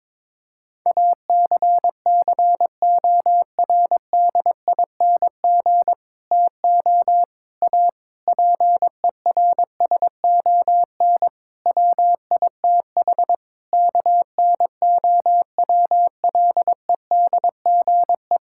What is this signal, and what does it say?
Morse code, heard as a keyed tone, about 22 wpm, 700 Hz. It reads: ACCORDING TO A PERSON WITH KNOWLEDGE
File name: practice / 22wpm / Sets of 8 Words - 22wpm